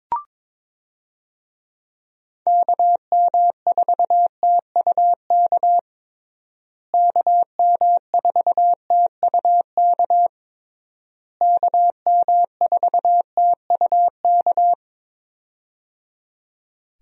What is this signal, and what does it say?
Morse code, heard as a keyed tone, about 22 wpm, 700 Hz. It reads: KM4TUK KM4TUK KM4TUK
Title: USA Call Signs - 6 Characters - Repeated 3x - 22wpm